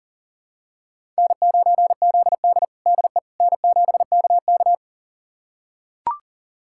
Morse code, heard as a keyed tone, about 40 wpm, 700 Hz. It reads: N9ZD DE N7KK
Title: USA Callsign DE Callsign - 4 Characters - 40wpm